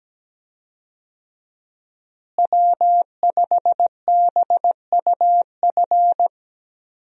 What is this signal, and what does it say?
Morse code, heard as a keyed tone, about 17 wpm, 700 Hz. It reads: W5BUF
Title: USA Call Signs - 5 Characters - Repeated 3x - 17wpm